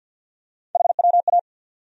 Morse code, heard as a keyed tone, about 51 wpm, 700 Hz. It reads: HWA
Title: USA Call Sign Postfixes - 3 Letters 50wpm